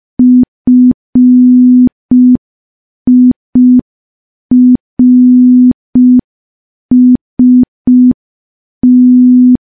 Morse code, heard as a keyed tone, about 5 wpm, 250 Hz. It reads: FIRST
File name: morse / morse-training-05WPM